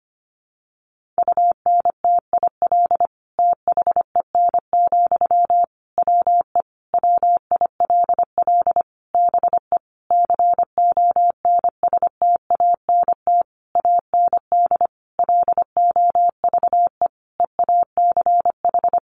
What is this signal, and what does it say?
Morse code, heard as a keyed tone, about 25 wpm, 700 Hz. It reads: UNTIL THEN, WE WILL BE CONSTANT AND LOVE EACH
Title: Sentences from Top 1500 Words 25wpm